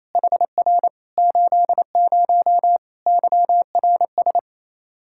Morse code, heard as a keyed tone, about 28 wpm, 700 Hz. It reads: HR 80 YRS
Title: K1USN SST QSO Practice 28wpm